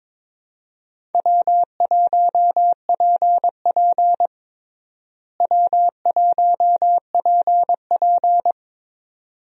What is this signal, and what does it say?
Morse code, heard as a keyed tone, about 22 wpm, 700 Hz. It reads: W1PP W1PP
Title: USA Call Signs - 4 Characters - Repeated 2x - 22wpm